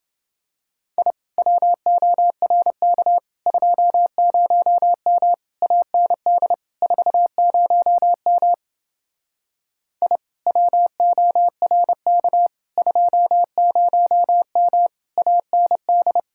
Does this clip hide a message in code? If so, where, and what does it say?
Yes: Morse code from 1.0 to 16.2 s, heard as a keyed tone, about 29 wpm, 700 Hz. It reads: I WORK 20M AND 40M I WORK 20M AND